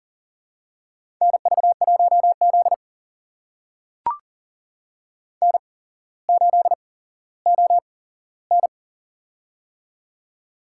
Morse code, heard as a keyed tone, about 40 wpm, spaced as for 17 wpm, 700 Hz. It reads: NV1Z N8ON
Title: USA Call Signs - 4 Characters - 8x Character Spacing - 40wpm